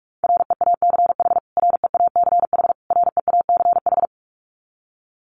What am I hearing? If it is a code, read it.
Morse code, heard as a keyed tone, about 44 wpm, 700 Hz. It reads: REACH REACH REACH
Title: Top 500 Words Repeated 45wpm